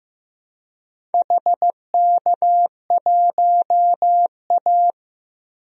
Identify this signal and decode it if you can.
Morse code, heard as a keyed tone, about 15 wpm, 700 Hz. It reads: HK1A
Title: DXCC-100 Call Signs - 4 Characters - Repeated 2x - 15wpm